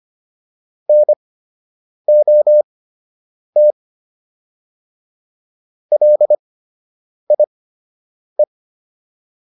Morse code, heard as a keyed tone, about 25 wpm, spaced as for 8 wpm, 600 Hz. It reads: NOT LIE